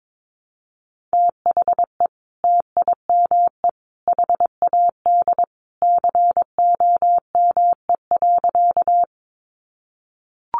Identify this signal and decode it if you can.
Morse code, heard as a keyed tone, about 22 wpm, 700 Hz. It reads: THE TIME HAD COME.
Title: Sentences from Top 100 Words 22wpm